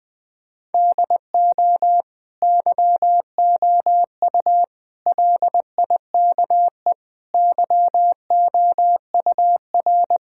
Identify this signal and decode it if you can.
Morse code, heard as a keyed tone, about 19 wpm, 700 Hz. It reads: DO YOU LIKE YOUR
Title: Sentences from Top 1000 Words - Review Part 5 - 20wpm